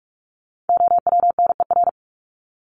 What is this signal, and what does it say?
Morse code, heard as a keyed tone, about 44 wpm, 700 Hz. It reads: OWNER
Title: Top 1501-1750 Words 45wpm